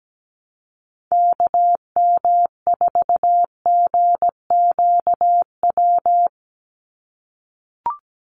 Morse code, heard as a keyed tone, about 17 wpm, 700 Hz. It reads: KM4GQW
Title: USA Call Signs 17wpm